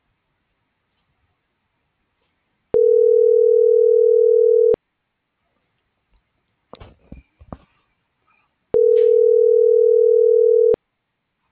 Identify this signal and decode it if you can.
Ringback tone (North America), 440 + 480 Hz.
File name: CENTRAL GOMSA / out alan.escudero